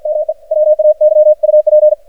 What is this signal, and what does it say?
Morse code, heard as a keyed tone, about 53 wpm, 610 Hz. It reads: ME WAY UP
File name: archived / CRSnet